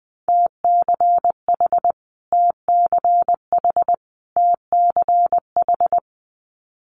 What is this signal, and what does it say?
Morse code, heard as a keyed tone, about 20 wpm, 700 Hz. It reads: TCH TCH TCH
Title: Multi-letter Phonemes Repeated 3x 20wpm